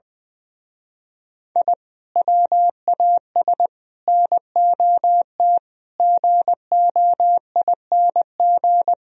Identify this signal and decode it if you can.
Morse code, heard as a keyed tone, about 21 wpm, 700 Hz. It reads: I WAS NOT GOING
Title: Speed-Racing - Sets of 5 Words 20wpm